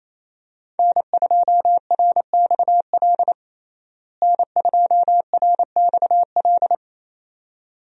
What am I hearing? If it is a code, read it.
Morse code, heard as a keyed tone, about 28 wpm, 700 Hz. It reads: N2RXL N2RXL
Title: USA Call Signs - 5 Characters - Repeated 2x - 28wpm